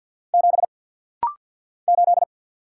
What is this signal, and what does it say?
Morse code, heard as a keyed tone, about 55 wpm, 700 Hz. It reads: 7 8
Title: Single Letter-Number Alt - Rapid-Fire 50wpm